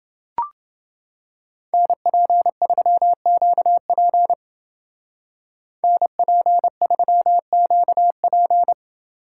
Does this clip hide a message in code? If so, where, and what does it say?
Yes: Morse code from 1.7 to 8.7 s, heard as a keyed tone, about 28 wpm, 700 Hz. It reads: NP3QP NP3QP